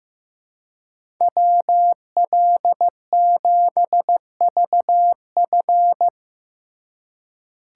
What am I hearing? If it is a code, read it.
Morse code, heard as a keyed tone, about 15 wpm, 700 Hz. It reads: WL7VF
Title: USA Call Signs - 5 Characters - 3x Character Spacing - 15wpm